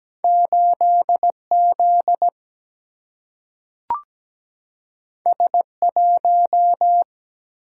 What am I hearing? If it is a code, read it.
Morse code, heard as a keyed tone, about 17 wpm, 700 Hz. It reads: 8 Z S 1
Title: Single-Letter Single-Number 17wpm